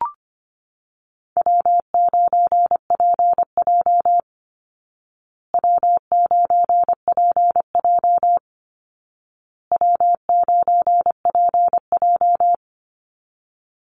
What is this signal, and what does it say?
Morse code, heard as a keyed tone, about 25 wpm, 700 Hz. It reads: W9PJ W9PJ W9PJ
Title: Easy Call Signs Repeated 3x - 25wpm